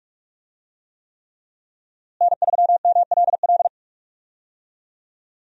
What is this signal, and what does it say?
Morse code, heard as a keyed tone, about 46 wpm, 700 Hz. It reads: N3MLL
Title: USA Call Signs - 5 Characters - Repeated 3x - 45wpm